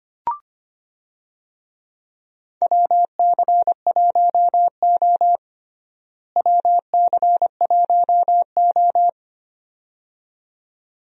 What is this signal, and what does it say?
Morse code, heard as a keyed tone, about 25 wpm, 700 Hz. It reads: WC1O WC1O
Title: USA Call Signs - 4 Characters - Repeated 2x - 25wpm